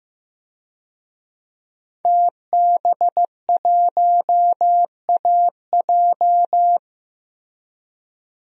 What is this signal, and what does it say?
Morse code, heard as a keyed tone, about 15 wpm, 700 Hz. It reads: TB1AJ
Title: DXCC-100 Call Signs - 5 Characters - 15wpm